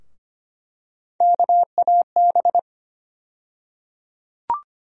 Morse code, heard as a keyed tone, about 25 wpm, 700 Hz. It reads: KAB
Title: USA Call Sign Postfixes - 3 Letters 25wpm